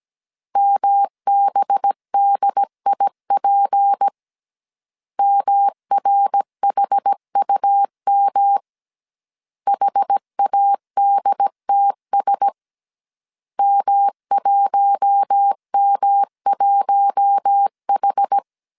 Morse code, heard as a keyed tone, about 17 wpm, 800 Hz. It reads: MBDIP MRHUM HADTS M1M1H
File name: CW / DET / 1 6 1500C S B 16wpm 2619